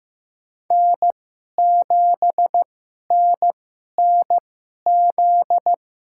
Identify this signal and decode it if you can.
Morse code, heard as a keyed tone, about 15 wpm, spaced as for 11 wpm, 700 Hz. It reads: N7NNZ
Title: USA Call Signs - 5 Characters - 2x Character Spacing - 15wpm